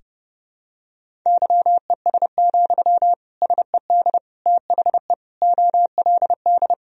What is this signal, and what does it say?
Morse code, heard as a keyed tone, about 30 wpm, 700 Hz. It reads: YES, SED THE OLD
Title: Sentences from Top 500 Words and Abbreviations 30wpm